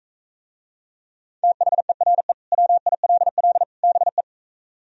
Morse code, heard as a keyed tone, about 45 wpm, 700 Hz. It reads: THERE WILL BE